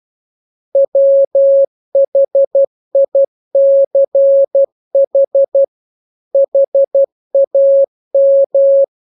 Morse code, heard as a keyed tone, about 12 wpm, 550 Hz. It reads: WHICH HAM